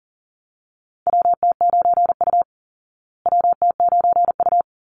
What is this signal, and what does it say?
Morse code, heard as a keyed tone, about 40 wpm, 700 Hz. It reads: WT9U WT9U
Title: Call Signs Repeated 3x - 40wpm